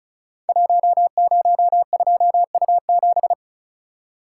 Morse code, heard as a keyed tone, about 36 wpm, 700 Hz. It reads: 102U7
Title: Code Groups - 5 Letter:Numbers - 35wpm